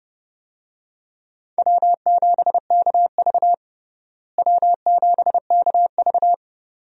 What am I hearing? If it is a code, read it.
Morse code, heard as a keyed tone, about 30 wpm, 700 Hz. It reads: W7KV W7KV
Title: USA Call Signs - 4 Characters - Repeated 2x - 30wpm